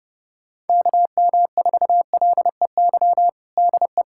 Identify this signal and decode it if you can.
Morse code, heard as a keyed tone, about 30 wpm, 700 Hz. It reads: KM4LEY DE
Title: USA Callsign DE Callsign 30wpm